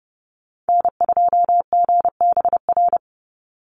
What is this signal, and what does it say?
Morse code, heard as a keyed tone, about 30 wpm, 700 Hz. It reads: N2GBR
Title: NA SOTA Activator Callsigns 30wpm